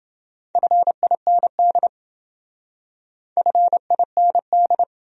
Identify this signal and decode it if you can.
Morse code, heard as a keyed tone, about 27 wpm, 700 Hz. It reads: FIND FIND